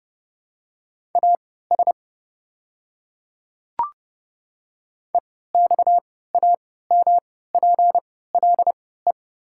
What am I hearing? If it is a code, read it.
Morse code, heard as a keyed tone, about 30 wpm, spaced as for 17 wpm, 700 Hz. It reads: AS EXAMPLE
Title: 3x Character Spacing - Top 300 Words - 30wpm